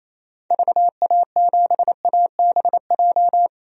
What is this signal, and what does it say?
Morse code, heard as a keyed tone, about 28 wpm, 700 Hz. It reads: VA7ABJ
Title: DXCC-100 Call Signs - 6 Characters - 28wpm